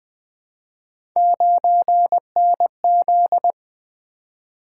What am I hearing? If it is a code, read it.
Morse code, heard as a keyed tone, about 20 wpm, 700 Hz. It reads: 9NZ
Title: Three Letter-Number Combinations - 20wpm